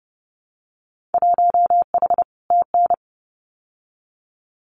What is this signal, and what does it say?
Morse code, heard as a keyed tone, about 30 wpm, 700 Hz. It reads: 1H TN
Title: Winter Field Day Exchanges 30wpm